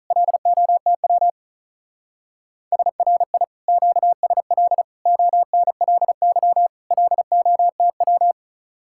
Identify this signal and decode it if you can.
Morse code, heard as a keyed tone, about 43 wpm, 700 Hz. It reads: LOTW SRI QSL ONLY LOTW
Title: Speed-Racing Common QSO Phrases - 35wpm